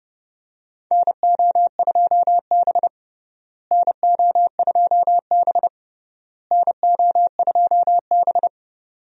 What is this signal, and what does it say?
Morse code, heard as a keyed tone, about 32 wpm, 700 Hz. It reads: NO2B NO2B NO2B
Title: USA Call Signs - 4 Characters - Repeated 3x - 30wpm